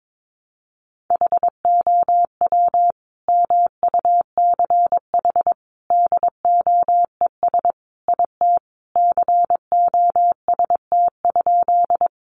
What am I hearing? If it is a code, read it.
Morse code, heard as a keyed tone, about 22 wpm, 700 Hz. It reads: HOW MUCH DOES IT COST?
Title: Sentences from Top 800 Words 22wpm